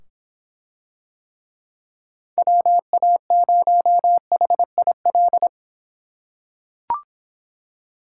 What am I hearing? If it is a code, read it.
Morse code, heard as a keyed tone, about 25 wpm, 700 Hz. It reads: WA0HIL